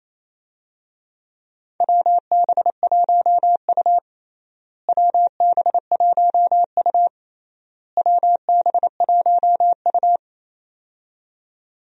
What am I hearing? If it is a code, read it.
Morse code, heard as a keyed tone, about 28 wpm, 700 Hz. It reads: WB1U WB1U WB1U
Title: USA Call Signs - 4 Characters - Repeated 3x - 28wpm